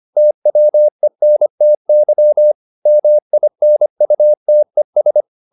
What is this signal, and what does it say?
Morse code, heard as a keyed tone, about 25 wpm, 600 Hz. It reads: TWENTY MINUTES